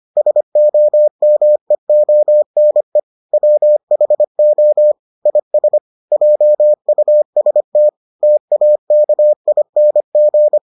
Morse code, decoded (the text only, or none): SOMEONE WHO IS JUST TAKING